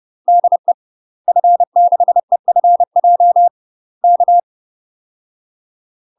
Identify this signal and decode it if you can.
Morse code, heard as a keyed tone, about 30 wpm, 700 Hz. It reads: DE F6EFJ K
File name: UK MASTER QSOs TRAINING / 30WPM / UK MASTER QSOs 0015